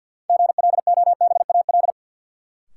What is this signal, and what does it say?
Morse code, heard as a keyed tone, about 53 wpm, 700 Hz. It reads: GLOBAL
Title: Sentences from Top 1200 Words - New Word Drill Alt - 50wpm